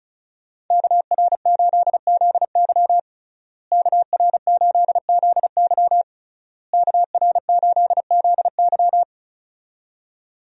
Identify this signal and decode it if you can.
Morse code, heard as a keyed tone, about 34 wpm, 700 Hz. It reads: KR8ZY KR8ZY KR8ZY
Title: USA Call Signs - 5 Characters - Repeated 3x - 35wpm